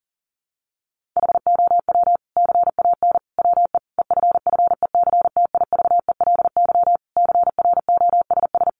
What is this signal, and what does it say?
Morse code, heard as a keyed tone, about 40 wpm, 700 Hz. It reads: HOW CAN WE EFFECTIVELY CROSS
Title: Sentences from Top 1400 Words 40wpm